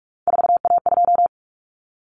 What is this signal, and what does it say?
Morse code, heard as a keyed tone, about 44 wpm, 700 Hz. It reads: 4A2
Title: Common Cut Numbers - 3-Digit 45wpm